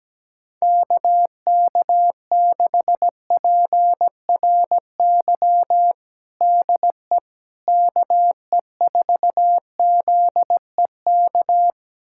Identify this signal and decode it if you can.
Morse code, heard as a keyed tone, about 17 wpm, 700 Hz. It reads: KK6PRY DE KE4ZEK
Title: USA Callsign DE Callsign - 6 Characters - 17wpm